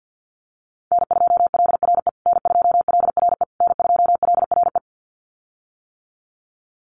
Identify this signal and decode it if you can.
Morse code, heard as a keyed tone, about 50 wpm, 700 Hz. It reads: N2LRE N2LRE N2LRE
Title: USA Call Signs Repeated 3x 50wpm